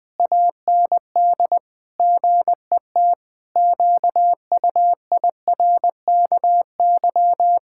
Morse code, heard as a keyed tone, about 20 wpm, 700 Hz. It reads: AND GET QUIRKY
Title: Pangram Challenge v2 - 20wpm